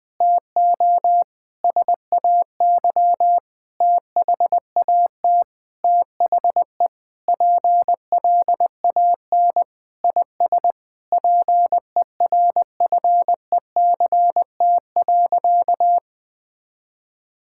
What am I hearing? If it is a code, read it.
Morse code, heard as a keyed tone, about 21 wpm, 700 Hz. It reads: TO SAY THAT THE PLAN IS PERFECT.
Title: Sentences with 7-Word Phrases - Spoken Beginning - 20wpm